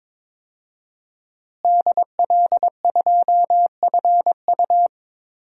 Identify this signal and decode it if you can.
Morse code, heard as a keyed tone, about 21 wpm, 700 Hz. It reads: DL2FU
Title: DXCC-100 Call Signs - 5 Characters - Repeated 2x - 22wpm